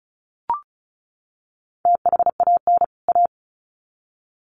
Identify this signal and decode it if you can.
Morse code, heard as a keyed tone, about 34 wpm, 700 Hz. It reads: THAN A
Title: Sets of 2 Words 35wpm